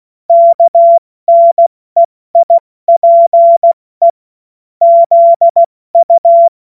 Morse code, heard as a keyed tone, about 16 wpm, spaced as for 14 wpm, 680 Hz. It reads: KNEIPE ZU
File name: Morsekurs / Witze / Witz 19. Oktober 2021, 70 F80